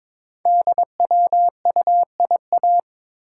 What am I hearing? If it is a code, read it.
Morse code, heard as a keyed tone, about 22 wpm, 700 Hz. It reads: DWUIA
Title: Code Groups - 5 Letter:Numbers - 22wpm